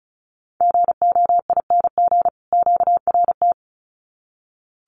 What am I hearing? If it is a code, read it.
Morse code, heard as a keyed tone, about 34 wpm, 700 Hz. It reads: GOING QRT
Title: Common QSO Phrases 35wpm